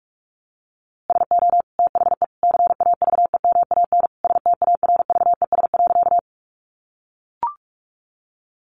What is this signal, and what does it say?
Morse code, heard as a keyed tone, about 44 wpm, 700 Hz. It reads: SO THE CAVEMAN STARVES.
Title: A Sound Of Thunder 45wpm